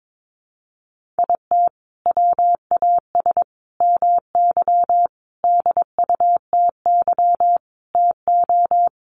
Morse code, heard as a keyed tone, about 22 wpm, 700 Hz. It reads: IT WAS MY DUTY TO
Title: Sentences from Top 1500 Words 22wpm